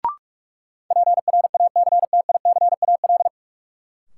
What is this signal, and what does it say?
Morse code, heard as a keyed tone, about 44 wpm, 700 Hz. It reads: PRACTICAL
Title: Sentences from Top 1400 Words - New Word Drill Alt - 45wpm